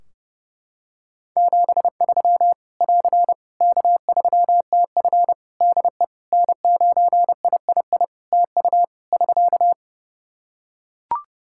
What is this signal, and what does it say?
Morse code, heard as a keyed tone, about 30 wpm, 700 Hz. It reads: 73 <AR> K3TF DE N9III TU <SK>
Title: Practice QSO - Exchange 4 - Ending - 30wpm